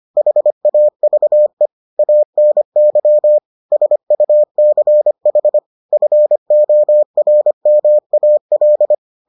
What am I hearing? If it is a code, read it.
Morse code, heard as a keyed tone, about 25 wpm, 600 Hz. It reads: HAVE ANY SUCH FORMAL